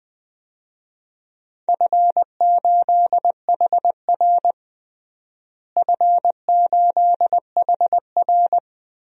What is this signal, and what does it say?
Morse code, heard as a keyed tone, about 20 wpm, 700 Hz. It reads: F8HR F8HR
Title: DXCC-100 Call Signs - 4 Characters - Repeated 3x - 20wpm